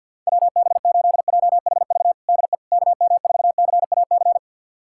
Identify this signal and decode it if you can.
Morse code, heard as a keyed tone, about 51 wpm, 700 Hz. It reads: WB8JHV DE KM4CAC
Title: USA Callsign DE Callsign 50wpm